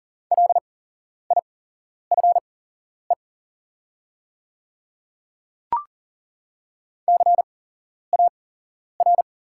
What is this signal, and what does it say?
Morse code, heard as a keyed tone, about 40 wpm, spaced as for 11 wpm, 700 Hz. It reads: LIFE CAR
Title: 8x Character Spacing - Top 300 Words - 40wpm